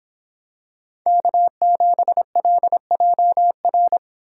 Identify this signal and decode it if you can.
Morse code, heard as a keyed tone, about 26 wpm, 700 Hz. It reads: K7LJR